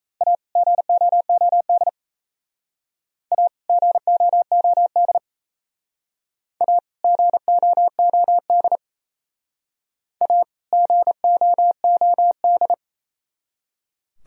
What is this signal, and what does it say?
Morse code, heard as a keyed tone, about 36 wpm, 700 Hz. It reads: A GOOD A GOOD A GOOD A GOOD